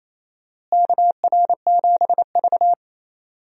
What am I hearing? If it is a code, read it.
Morse code, heard as a keyed tone, about 27 wpm, 700 Hz. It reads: KR7V
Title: USA Call Signs - 4 Characters - Repeated 2x - 28wpm